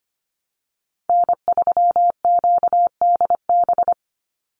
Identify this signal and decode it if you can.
Morse code, heard as a keyed tone, about 25 wpm, 700 Hz. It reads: N3QDB
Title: USA Call Signs 25wpm